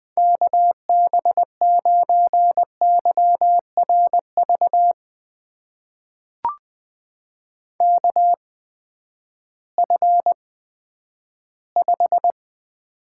Morse code, heard as a keyed tone, about 20 wpm, spaced as for 11 wpm, 700 Hz. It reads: KB9YRV KF5
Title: USA Call Signs - 6 Characters - 8x Character Spacing - 20wpm